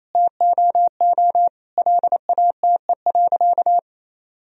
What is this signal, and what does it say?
Morse code, heard as a keyed tone, about 28 wpm, 700 Hz. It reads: TOO LATE.
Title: Sentences from Top 1000 Words - Review Part 4 - 28wpm